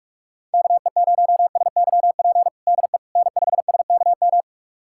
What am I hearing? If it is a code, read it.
Morse code, heard as a keyed tone, about 46 wpm, 700 Hz. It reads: KE0SYP DE N5SKM
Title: USA Callsign DE Callsign 45wpm